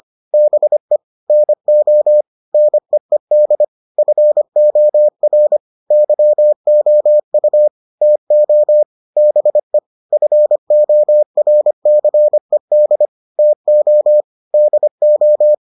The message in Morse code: BE NO NEED FOR YOU TO BE FORCED TO DO

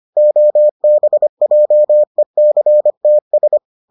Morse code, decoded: OBJECTS

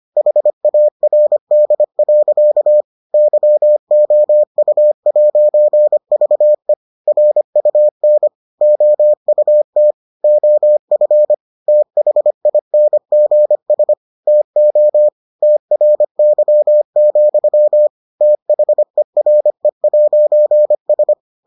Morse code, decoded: HARD. YOU'VE RUN OUT OF THINGS TO TRY, THERE'S

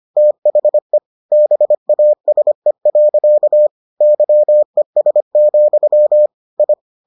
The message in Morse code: THE BASE. YES, I

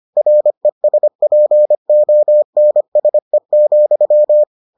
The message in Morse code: RESPONSE,

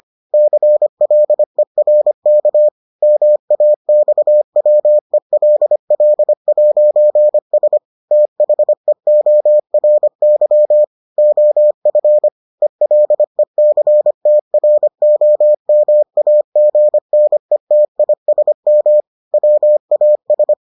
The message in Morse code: CLERK MAXWELL'S THEORY OF ELECTROMAGNETISM WAS

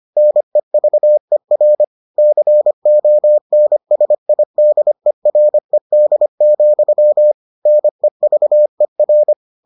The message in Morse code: NEVER CONSIDERED, NEVER

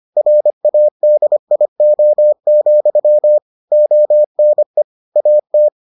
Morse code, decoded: RADIO, ONE AT

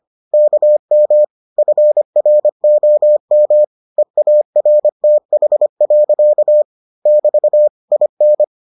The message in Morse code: KM FROM EARTH. <BT> IN

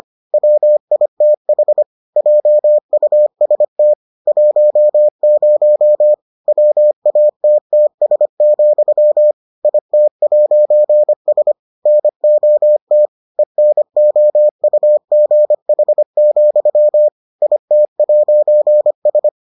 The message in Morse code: WITH JUST 10 WATTS, IT'S NOT ENOUGH, IT'S

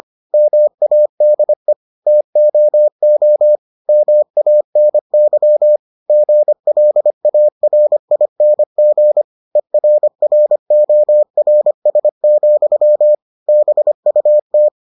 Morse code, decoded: MADE TOO MANY GLARING ERRORS, BUT

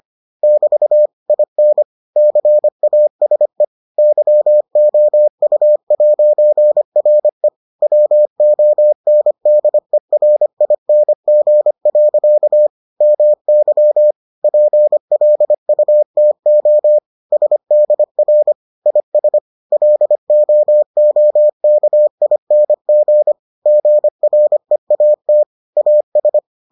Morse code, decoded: <BT> IN CASE YOU'RE WONDERING. MY PLUTO SDR IS LOOKING GREAT AS

